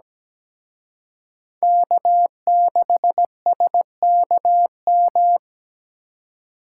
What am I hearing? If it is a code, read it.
Morse code, heard as a keyed tone, about 20 wpm, 700 Hz. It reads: K6SKM
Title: Speed Racing - USA 1x3 Call Signs 17wpm